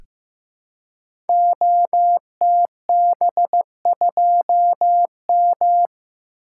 Morse code, heard as a keyed tone, about 15 wpm, 700 Hz. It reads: OTB2M
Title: Code Groups - 5 Letter:Numbers - 15wpm